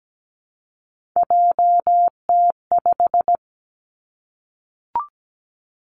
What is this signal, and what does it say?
Morse code, heard as a keyed tone, about 17 wpm, 700 Hz. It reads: JT5
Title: Three Challenging Letter-Number Combinations 17wpm